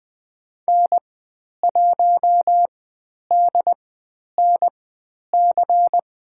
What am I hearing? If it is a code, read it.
Morse code, heard as a keyed tone, about 20 wpm, spaced as for 10 wpm, 700 Hz. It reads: N1DNC
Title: 095 Lesson 16g - Letter c - Call Signs 20wpm